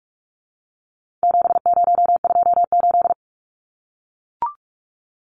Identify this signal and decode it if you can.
Morse code, heard as a keyed tone, about 44 wpm, 700 Hz. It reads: 7028
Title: Numbers - 4-Digit 45wpm